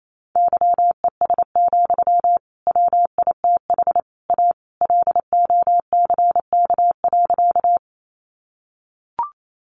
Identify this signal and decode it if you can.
Morse code, heard as a keyed tone, about 28 wpm, 700 Hz. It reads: YES, WITH A LOCK.
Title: Sentences from Top 1300 Words 28wpm